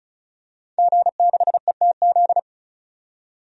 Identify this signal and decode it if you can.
Morse code, heard as a keyed tone, about 34 wpm, 700 Hz. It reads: G6ETZ
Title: DXCC-100 Call Signs - 5 Characters - 35wpm